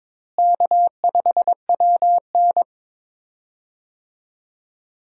Morse code, heard as a keyed tone, about 22 wpm, 700 Hz. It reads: K5WN
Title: USA Call Signs - 4 Characters - Repeated 3x - 22wpm